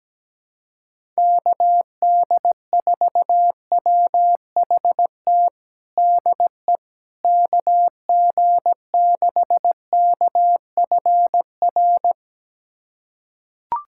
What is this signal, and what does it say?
Morse code, heard as a keyed tone, about 17 wpm, 700 Hz. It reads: KD4WHT DE KG6KFR
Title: USA Callsign DE Callsign - 6 Characters - 17wpm